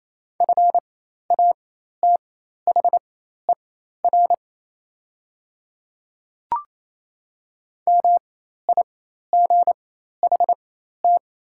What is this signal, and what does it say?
Morse code, heard as a keyed tone, about 28 wpm, spaced as for 13 wpm, 700 Hz. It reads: FATHER MIGHT
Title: 4x Character Spacing - Top 300 Words - 28wpm